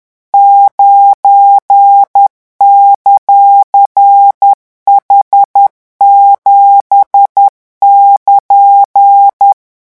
Morse code, heard as a keyed tone, about 10 wpm, 800 Hz. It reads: 9;H7<KN>
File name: morse mp3 / revision16-vitesse 10 mots